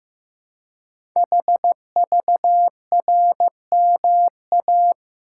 Morse code, heard as a keyed tone, about 15 wpm, 700 Hz. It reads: HVRMA
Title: Code Groups - 5 Letters - 15wpm